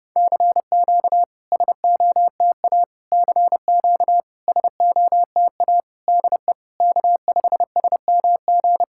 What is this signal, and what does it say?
Morse code, heard as a keyed tone, about 32 wpm, 700 Hz. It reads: CQ SOTA CQ SOTA DE K5SMG
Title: SOTA QSO Practice 30wpm